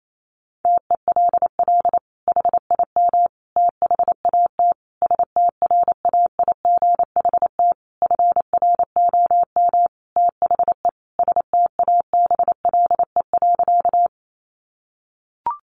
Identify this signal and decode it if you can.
Morse code, heard as a keyed tone, about 28 wpm, 700 Hz. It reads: TELL HIM THAT STRAIGHT FROM THE STABLE.
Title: Sentences from Top 1500 Words 28wpm